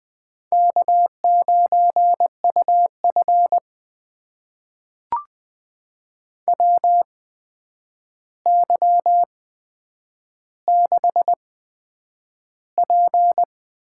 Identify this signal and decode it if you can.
Morse code, heard as a keyed tone, about 20 wpm, spaced as for 9 wpm, 700 Hz. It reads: K9UF WY6P
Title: USA Call Signs - 4 Characters - 8x Character Spacing - 20wpm